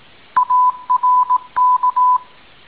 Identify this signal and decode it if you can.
Morse code, heard as a keyed tone, about 18 wpm, 1000 Hz. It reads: ARK